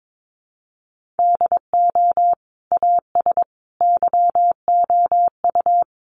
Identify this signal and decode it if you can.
Morse code, heard as a keyed tone, about 22 wpm, 700 Hz. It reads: DO AS YOU
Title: Sentences from Top 1500 Words 22wpm